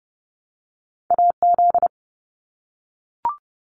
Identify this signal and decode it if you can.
Morse code, heard as a keyed tone, about 30 wpm, 700 Hz. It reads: AZ
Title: US States 30wpm